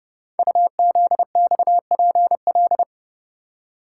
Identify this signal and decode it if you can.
Morse code, heard as a keyed tone, about 30 wpm, 700 Hz. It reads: UZXPL
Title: Code Groups - 5 Letters - 30wpm